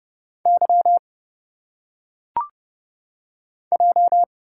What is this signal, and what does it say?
Morse code, heard as a keyed tone, about 30 wpm, 700 Hz. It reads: Y J
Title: Single Letters - Phonetic Pronunciation 30wpm